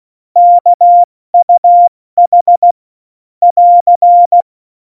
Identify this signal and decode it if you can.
Morse code, heard as a keyed tone, about 16 wpm, spaced as for 14 wpm, 700 Hz. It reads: KUH <AR>
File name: Morsekurs / Witze / Witz 20. November 2018, 70 F80